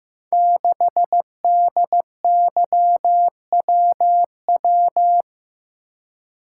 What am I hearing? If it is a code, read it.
Morse code, heard as a keyed tone, about 15 wpm, 700 Hz. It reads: 6DYWW
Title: Code Groups - 5 Letter:Numbers - 15wpm